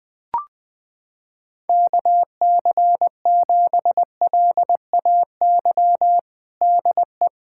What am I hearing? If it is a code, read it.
Morse code, heard as a keyed tone, about 21 wpm, 700 Hz. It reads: KC7LAY DE
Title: USA Callsign DE Callsign - 6 Characters - 20wpm